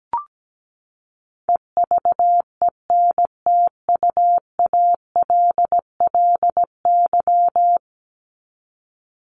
Morse code, heard as a keyed tone, about 17 wpm, 700 Hz. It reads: EVENTUALLY
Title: LY Suffix - Top 100 Words with Base Words 17wpm